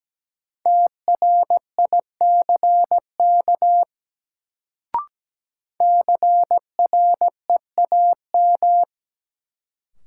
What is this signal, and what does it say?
Morse code, heard as a keyed tone, about 17 wpm, 700 Hz. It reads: TRICK CREAM
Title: Sentences from Top 1500 Words - New Word Drill - 17wpm